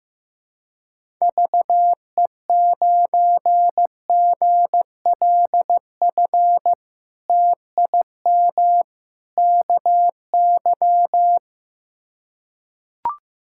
Morse code, heard as a keyed tone, about 15 wpm, 700 Hz. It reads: VE9GLF TIM KY
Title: K1USN SST QSO Practice - 15wpm